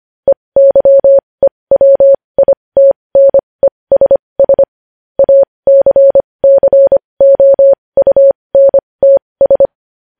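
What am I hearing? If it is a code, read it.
Morse code, heard as a keyed tone, about 25 wpm, spaced as for 20 wpm, 550 Hz. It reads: EYEWITNESS ACCOUNTS
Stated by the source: Morse code news headlines sent at 20.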